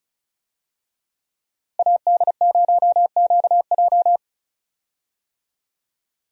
Morse code, heard as a keyed tone, about 34 wpm, 700 Hz. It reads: AD0QJ
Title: USA Call Signs - 5 Characters - Repeated 2x - 35wpm